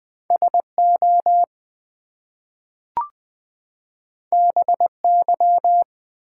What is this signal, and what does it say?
Morse code, heard as a keyed tone, about 20 wpm, 700 Hz. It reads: SO BY
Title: Two Letter Words - 20wpm